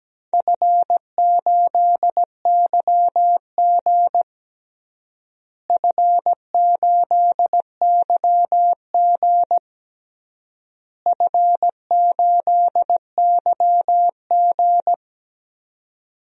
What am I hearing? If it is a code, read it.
Morse code, heard as a keyed tone, about 17 wpm, 700 Hz. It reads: F8YG F8YG F8YG
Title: DXCC-100 Call Signs - 4 Characters - Repeated 3x - 17wpm